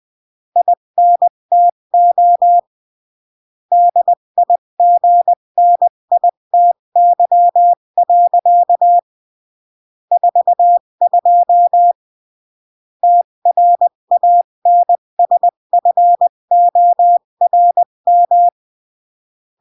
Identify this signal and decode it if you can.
Morse code, heard as a keyed tone, about 20 wpm, 700 Hz. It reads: INTO DIGNITY. 42 TRANSFORM
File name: CW / jodorowsky / 82 commandments 20-15